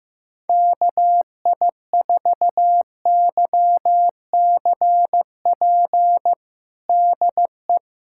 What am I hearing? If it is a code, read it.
Morse code, heard as a keyed tone, about 15 wpm, 700 Hz. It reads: KI4YCP DE
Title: USA Callsign DE Callsign - 6 Characters - 15wpm